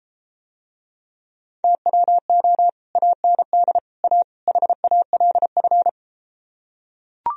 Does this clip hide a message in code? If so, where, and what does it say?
Yes: Morse code from 1.6 to 5.9 s, heard as a keyed tone, about 33 wpm, 700 Hz. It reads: TWO AND A HALF